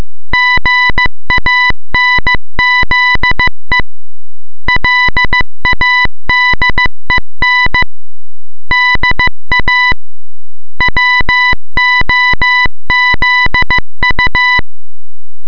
Morse code, heard as a keyed tone, about 15 wpm, 2000 Hz. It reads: GANZE LADEN DA WOZU